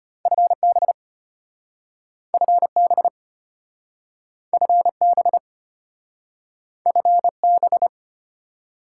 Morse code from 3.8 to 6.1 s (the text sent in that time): FB